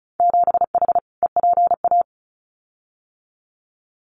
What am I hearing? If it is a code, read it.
Morse code, heard as a keyed tone, about 36 wpm, 700 Hz. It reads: 7H EPA
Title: Winter Field Day Exchanges Encore 35wpm